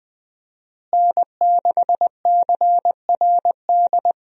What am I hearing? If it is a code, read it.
Morse code, heard as a keyed tone, about 20 wpm, 700 Hz. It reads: N6CRD
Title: USA Call Signs - 5 Characters - Repeated 3x - 20wpm